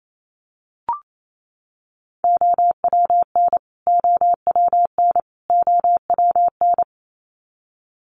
Morse code, heard as a keyed tone, about 28 wpm, 700 Hz. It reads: OWN OWN OWN
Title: Top 500 Words Repeated 28wpm